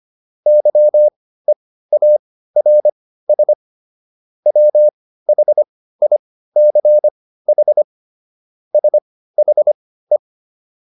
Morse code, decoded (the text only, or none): YEARS WHICH SHE